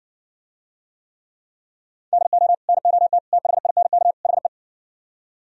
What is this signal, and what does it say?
Morse code, heard as a keyed tone, about 59 wpm, 700 Hz. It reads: DO NOT THINK HE